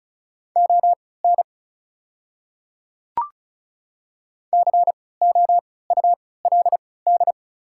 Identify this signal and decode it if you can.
Morse code, heard as a keyed tone, about 34 wpm, spaced as for 20 wpm, 690 Hz. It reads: ON COULD
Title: 3x Character Spacing - Top 100 Words - 35wpm